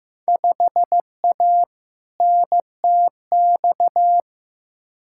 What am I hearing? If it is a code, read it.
Morse code, heard as a keyed tone, about 15 wpm, 700 Hz. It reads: 5A NTX
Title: Field Day Exchanges - 15wpm